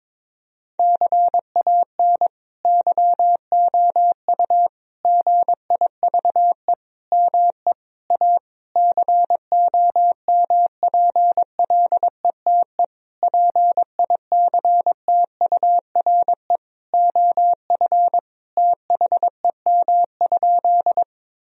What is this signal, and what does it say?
Morse code, heard as a keyed tone, about 23 wpm, 700 Hz. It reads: CAN YOU GIVE ME A COMPLETE PICTURE OF THEM?
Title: Sentences from Top 1000 Words - Review Part 2 - 22wpm